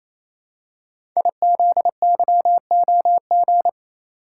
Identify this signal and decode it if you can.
Morse code, heard as a keyed tone, about 27 wpm, 700 Hz. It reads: IZYOG
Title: Code Groups - 5 Letters - 28wpm